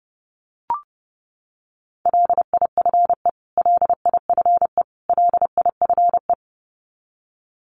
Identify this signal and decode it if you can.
Morse code, heard as a keyed tone, about 30 wpm, 700 Hz. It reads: LIFE LIFE LIFE
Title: Top 500 Words Repeated 30wpm